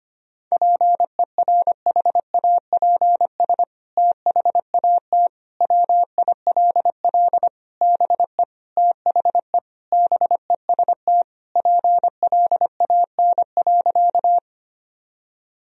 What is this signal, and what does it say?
Morse code, heard as a keyed tone, about 25 wpm, 700 Hz. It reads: PERHAPS THAT WILL BE THE BEST PLAN.
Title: Sentences from Top 1000 Words - Review Part 2 - 25wpm